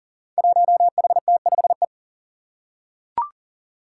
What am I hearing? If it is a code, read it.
Morse code, heard as a keyed tone, about 38 wpm, 700 Hz. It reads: 1HT5E
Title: Five Challenging Letter-Number Combinations 40wpm